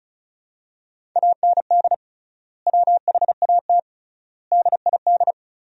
Morse code, heard as a keyed tone, about 36 wpm, 700 Hz. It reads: AND WHAT DID
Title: 3x Word-Spacing - Sentences from Top 300 Words - 35wpm